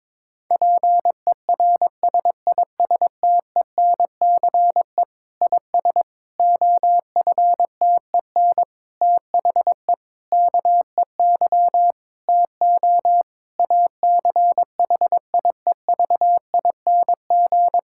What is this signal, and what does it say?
Morse code, heard as a keyed tone, about 22 wpm, 700 Hz. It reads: PERSISTENCE IS OFTEN THE KEY TO ACHIEVING
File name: Difficult Sentences with Hints - 22wpm